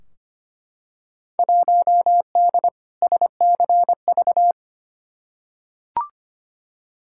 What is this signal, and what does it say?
Morse code, heard as a keyed tone, about 25 wpm, 700 Hz. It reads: 1D SCV
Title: Field Day Exchanges Encore - 25wpm